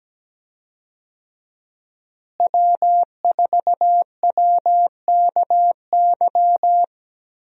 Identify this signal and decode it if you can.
Morse code, heard as a keyed tone, about 17 wpm, 700 Hz. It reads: W4WKY
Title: USA Call Signs - 5 Characters - Repeated 2x - 17wpm